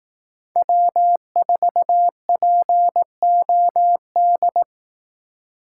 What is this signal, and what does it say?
Morse code, heard as a keyed tone, about 18 wpm, 700 Hz. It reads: W4POD
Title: Speed Racing - USA 1x3 Call Signs 15wpm